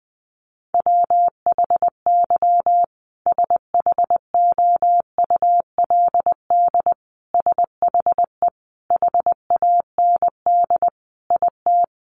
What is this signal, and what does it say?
Morse code, heard as a keyed tone, about 20 wpm, 700 Hz. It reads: WHY SHOULD SHE HAND IT
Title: Sentences from Top 900 Words 20wpm